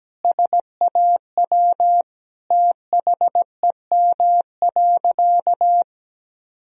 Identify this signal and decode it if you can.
Morse code, heard as a keyed tone, about 17 wpm, 700 Hz. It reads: SAW THEM.
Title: Sentences from Top 1000 Words - Review Part 2 - 17wpm